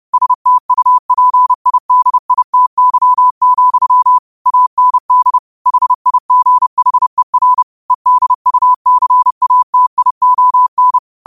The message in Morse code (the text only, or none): STUPIDITY, AND HIGHER EDUCATION